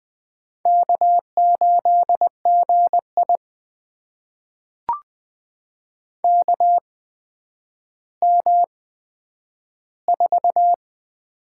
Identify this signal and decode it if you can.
Morse code, heard as a keyed tone, about 20 wpm, spaced as for 10 wpm, 700 Hz. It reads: K8GI KM4
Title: USA Call Signs - 8x Character Spacing - 20wpm